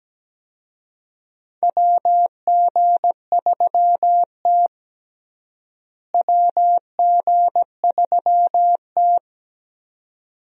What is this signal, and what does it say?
Morse code, heard as a keyed tone, about 17 wpm, 700 Hz. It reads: WG3T WG3T
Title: USA Call Signs - 4 Characters - Repeated 2x - 17wpm